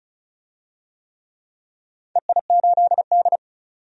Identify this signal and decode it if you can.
Morse code, heard as a keyed tone, about 36 wpm, 700 Hz. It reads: EI8D
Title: DXCC-100 Call Signs - 4 Characters - 35wpm